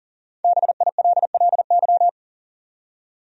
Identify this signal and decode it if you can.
Morse code, heard as a keyed tone, about 40 wpm, 700 Hz. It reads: BILLY
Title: USA Common Names - 40wpm